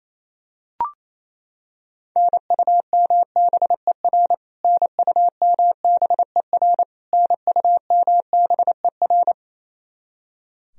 Morse code, heard as a keyed tone, about 28 wpm, 700 Hz. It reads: NUMBER NUMBER NUMBER
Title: Top 100 Words Repeated 28wpm